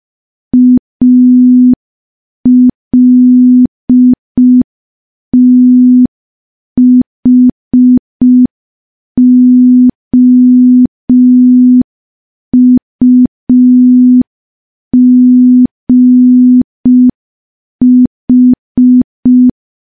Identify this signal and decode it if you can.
Morse code, heard as a keyed tone, about 5 wpm, 250 Hz. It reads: ALTHOUGH
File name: morse / morse-training-05WPM